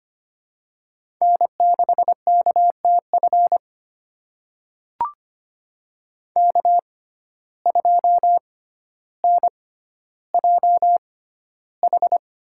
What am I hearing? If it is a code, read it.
Morse code, heard as a keyed tone, about 25 wpm, spaced as for 13 wpm, 700 Hz. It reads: N6KTF K2NJH
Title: USA Call Signs - 5 Characters - 6x Character Spacing - 25wpm